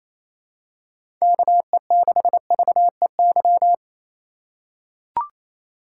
Morse code, heard as a keyed tone, about 28 wpm, 700 Hz. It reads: KE6VEY
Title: USA Call Signs - 2x Character Spacing - 28wpm